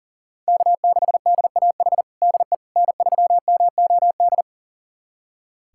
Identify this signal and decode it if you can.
Morse code, heard as a keyed tone, about 40 wpm, 700 Hz. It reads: K6DAH DE N3MOD
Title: USA Callsign DE Callsign - 5 Characters - 40wpm